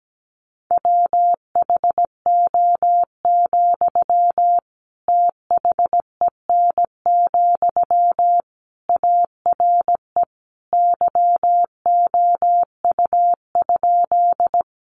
Morse code, heard as a keyed tone, about 17 wpm, 700 Hz. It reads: WHO, THEN, ARE YOU?
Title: Sentences from Top 100 Words 17wpm